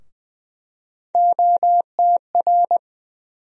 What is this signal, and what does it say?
Morse code, heard as a keyed tone, about 20 wpm, 700 Hz. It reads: OTR
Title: USA Call Sign Postfixes - 3 Letters 20wpm